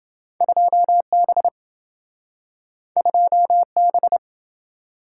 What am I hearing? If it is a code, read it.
Morse code, heard as a keyed tone, about 27 wpm, 700 Hz. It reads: 2B 2B